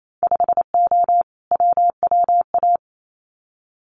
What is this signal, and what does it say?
Morse code, heard as a keyed tone, about 28 wpm, 700 Hz. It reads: 5O WWA
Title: Winter Field Day Exchanges Encore 28wpm